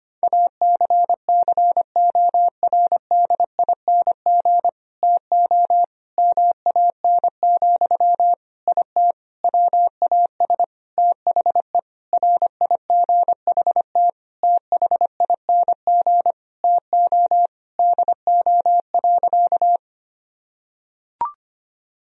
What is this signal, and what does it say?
Morse code, heard as a keyed tone, about 25 wpm, 700 Hz. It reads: ACCORDING TO MAN, IT WAS THE RIGHT THING TO DO.
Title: Sentences from Top 1000 Words - Review Part 4 - 25wpm